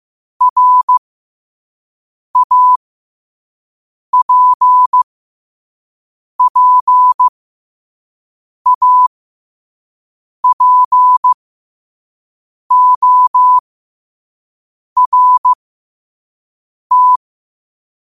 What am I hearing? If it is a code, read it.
Morse code, heard as a keyed tone, about 15 wpm, spaced as for 5 wpm, 1000 Hz. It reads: RAPPAPORT
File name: qotdfiles / 5wpm